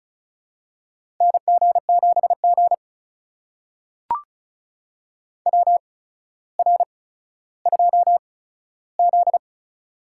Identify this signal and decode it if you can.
Morse code, heard as a keyed tone, about 36 wpm, spaced as for 15 wpm, 700 Hz. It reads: NG7G WR2Z
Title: USA Call Signs - 4 Characters - 8x Character Spacing - 35wpm